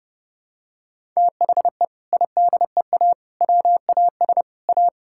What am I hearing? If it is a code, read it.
Morse code, heard as a keyed tone, about 30 wpm, 700 Hz. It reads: THE IDEA WAS A
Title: Sentences from Top 1000 Words - Review Part 6 - 30wpm